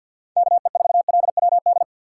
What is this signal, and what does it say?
Morse code, heard as a keyed tone, about 48 wpm, 700 Hz. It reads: KE4LWD
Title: USA Call Signs - 6 Characters - 6x Character Spacing - 50wpm